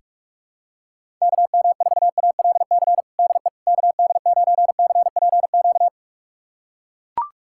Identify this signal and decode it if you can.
Morse code, heard as a keyed tone, about 41 wpm, 700 Hz. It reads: KM4ALC DE KD9CPQ
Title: USA Callsign DE Callsign 45wpm